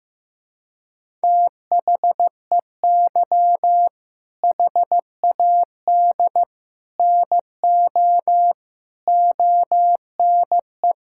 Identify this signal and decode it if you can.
Morse code, heard as a keyed tone, about 15 wpm, 700 Hz. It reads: THEY HAD NO ONE
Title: Sentences from Top 1600 Words 15wpm